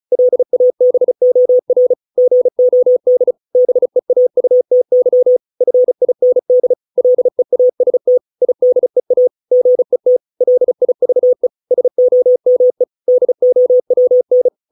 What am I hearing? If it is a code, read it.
Morse code, heard as a keyed tone, about 34 wpm, 500 Hz. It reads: LABOR GOD BEAUTY FIND LEAST IDEA GET LIVE SOME DOWN